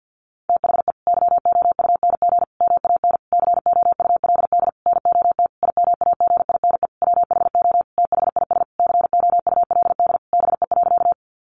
Morse code, heard as a keyed tone, about 50 wpm, 700 Hz. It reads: THE YOUNG MAN COULD NOT IMAGINE WHO THIS COULD BE.
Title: Sentences from Top 1400 Words 50wpm